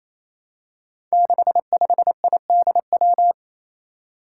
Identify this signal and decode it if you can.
Morse code, heard as a keyed tone, about 29 wpm, 700 Hz. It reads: 65IDW
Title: Five Challenging Letter-Number Combinations 28wpm